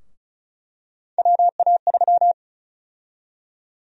Morse code, heard as a keyed tone, about 36 wpm, 700 Hz. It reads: WA3
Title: USA Call Sign Prefixes - Repeated 2x - 35wpm